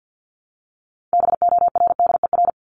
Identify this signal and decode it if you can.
Morse code, heard as a keyed tone, about 50 wpm, 700 Hz. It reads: BORDER
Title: Top 1751-2000 Words 50wpm